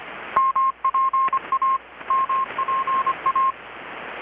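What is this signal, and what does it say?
Morse code, heard as a keyed tone, about 25 wpm, 1070 Hz. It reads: MPA MPA
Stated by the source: Morse from KPH's Night of Nights XX maritime radiotelegraphy event, received in Chiapas, Mexico about 3,800 km away on a Kenwood R-600 with a grounded 10 m long-wire antenna.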